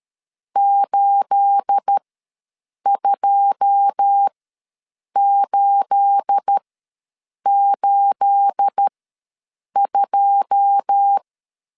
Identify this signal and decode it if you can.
Morse code, heard as a keyed tone, about 13 wpm, spaced as for 7 wpm, 800 Hz. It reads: 82882
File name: LEZIONI / 9 28 ws 12wpm 3354